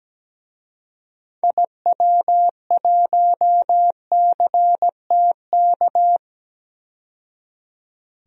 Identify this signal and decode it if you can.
Morse code, heard as a keyed tone, about 17 wpm, 700 Hz. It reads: IW1CTK
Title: DXCC-100 Call Signs - 6 Characters - 17wpm